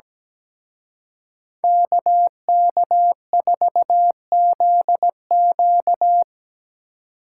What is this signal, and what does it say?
Morse code, heard as a keyed tone, about 20 wpm, 700 Hz. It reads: KK4ZQ
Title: Speed Racing - USA Call Signs 17wpm